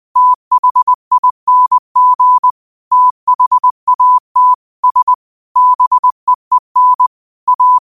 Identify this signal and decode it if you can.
Morse code, heard as a keyed tone, about 20 wpm, 1000 Hz. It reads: THING THAT S BEEN A
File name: qotdfiles / 20wpm